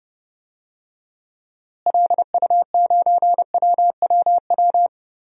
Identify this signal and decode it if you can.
Morse code, heard as a keyed tone, about 30 wpm, 700 Hz. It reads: LU9WWW
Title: DXCC-100 Call Signs - 6 Characters - Repeated 2x - 30wpm